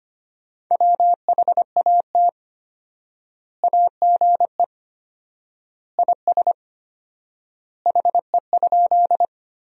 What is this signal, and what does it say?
Morse code, heard as a keyed tone, about 25 wpm, 700 Hz. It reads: WHAT AGE IS HE?
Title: 4x Word-Spacing - Sentences from Top 500 Words - 25wpm